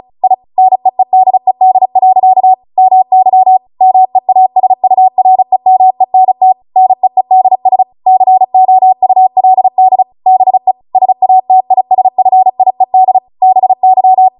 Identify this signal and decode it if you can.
Morse code, heard as a keyed tone, about 36 wpm, 750 Hz. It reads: I NEEDED. MY MEASUREMENT NEEDS COULD BE SATISFIED BY